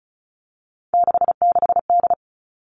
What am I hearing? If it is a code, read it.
Morse code, heard as a keyed tone, about 36 wpm, 700 Hz. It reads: 66D
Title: Three Challenging Letter-Number Combinations 35wpm